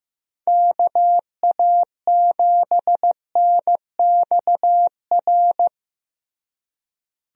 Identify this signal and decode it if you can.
Morse code, heard as a keyed tone, about 15 wpm, 700 Hz. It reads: KA7NXR
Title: USA Call Signs - 6 Characters - Repeated 2x - 15wpm